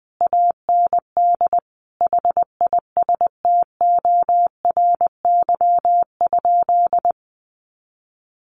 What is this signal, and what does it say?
Morse code, heard as a keyed tone, about 20 wpm, 700 Hz. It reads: AND HISTORY?
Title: Sentences from Top 600 Words 20wpm